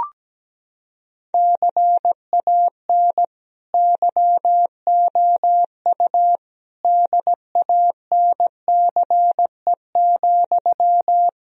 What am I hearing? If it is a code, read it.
Morse code, heard as a keyed tone, about 17 wpm, 700 Hz. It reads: CAN YOU DANCE,
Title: Sentences from Top 1000 Words - Review Part 4 - 17wpm